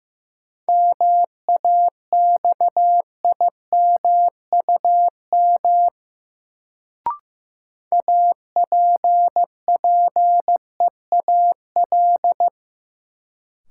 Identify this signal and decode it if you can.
Morse code, heard as a keyed tone, about 15 wpm, 700 Hz. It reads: MAXIMUM APPEAL
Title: Sentences from Top 1300 Words - New Word Drill - 15wpm